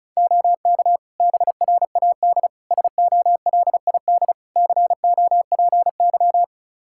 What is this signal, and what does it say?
Morse code, heard as a keyed tone, about 34 wpm, 700 Hz. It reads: OK BRAD SOLID COPY
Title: K1USN SST QSO Practice 35wpm